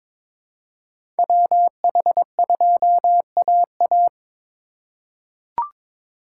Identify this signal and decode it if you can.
Morse code, heard as a keyed tone, about 22 wpm, 700 Hz. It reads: WH2AA
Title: USA Call Signs - 5 Characters - 2x Character Spacing - 22wpm